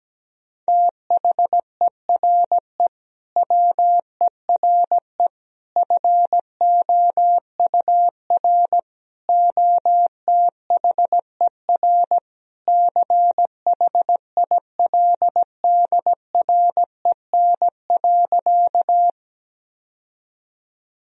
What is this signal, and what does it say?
Morse code, heard as a keyed tone, about 17 wpm, 700 Hz. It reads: THERE WERE FOUR OTHER CHILDREN.
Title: Sentences from Top 1000 Words - Review Part 1 - 17wpm